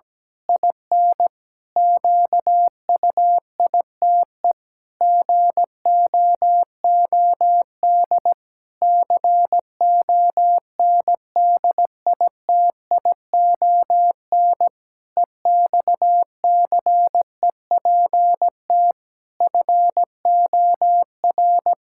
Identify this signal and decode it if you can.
Morse code, heard as a keyed tone, about 17 wpm, 700 Hz. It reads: IN QUITE GOOD CONDITION EXCEPT FOR
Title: Pangram Challenge v2 - 17wpm